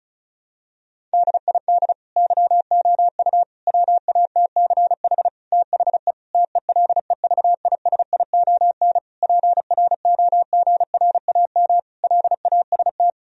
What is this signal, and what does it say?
Morse code, heard as a keyed tone, about 36 wpm, 700 Hz. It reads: DID YOU WATCH THE TELEVISION PROGRAM LAST
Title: Sentences from Top 1000 Words - Review Part 4 - 35wpm